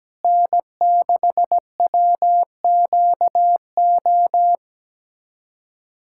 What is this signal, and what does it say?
Morse code, heard as a keyed tone, about 17 wpm, 700 Hz. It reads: N6WQO
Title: USA Callsign DE Callsign 17wpm